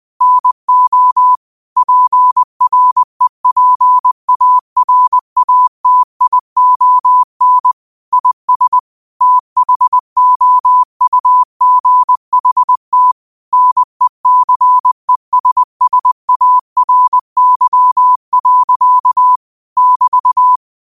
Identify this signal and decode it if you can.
Morse code, heard as a keyed tone, about 20 wpm, 1000 Hz. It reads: NO PREPARATION IS THOUGHT NECESSARY. <BT>
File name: qotdfiles / 20wpm